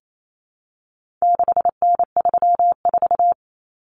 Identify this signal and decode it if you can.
Morse code, heard as a keyed tone, about 28 wpm, 700 Hz. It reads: 6N34
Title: Common Cut Numbers - 4-Digit 28wpm